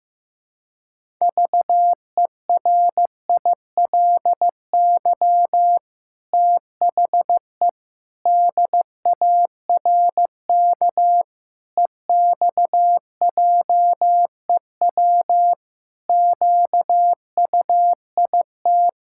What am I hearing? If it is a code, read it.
Morse code, heard as a keyed tone, about 15 wpm, 700 Hz. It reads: VERILY THE DARK EXJEW QUIT
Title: Pangram Challenge v2 - 15wpm